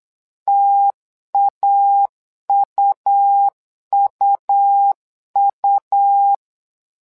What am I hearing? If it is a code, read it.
Morse code, heard as a keyed tone, about 8 wpm, 800 Hz. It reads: TAUUU
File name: LEZIONI / 2 UHDN / 2 U 8wpm 2028